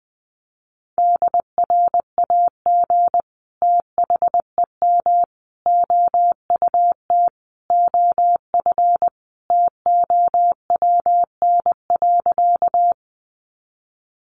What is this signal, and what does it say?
Morse code, heard as a keyed tone, about 20 wpm, 700 Hz. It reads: DRAG THEM OUT OF TOWN.
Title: Sentences from Top 1500 Words 20wpm